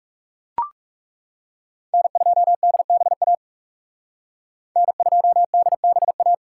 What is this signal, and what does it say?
Morse code, heard as a keyed tone, about 38 wpm, 700 Hz. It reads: N2DBA N2DBA